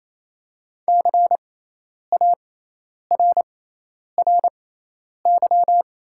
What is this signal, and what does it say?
Morse code, heard as a keyed tone, about 28 wpm, spaced as for 10 wpm, 700 Hz. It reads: CARRY
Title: 6x Character Spacing - Top 300 Words - 28wpm